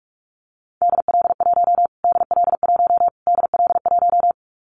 Morse code, heard as a keyed tone, about 44 wpm, 700 Hz. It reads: DL1 DL1 DL1
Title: Callsign Prefixes for Top 25 DXCC - Repeated 3x - 45wpm